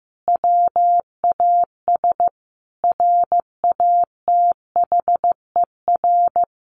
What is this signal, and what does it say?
Morse code, heard as a keyed tone, about 15 wpm, 700 Hz. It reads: WAS RATHER
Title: Sentences from Top 1400 Words 15wpm